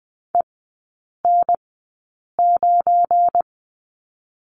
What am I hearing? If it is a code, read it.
Morse code, heard as a keyed tone, about 20 wpm, spaced as for 8 wpm, 700 Hz. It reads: EN9
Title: 184 Lesson 30e - Letter b - Three Characters 20wpm